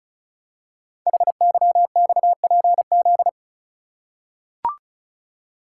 Morse code, heard as a keyed tone, about 36 wpm, 700 Hz. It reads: HYXPZ
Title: Code Groups - 5 Letters - 35wpm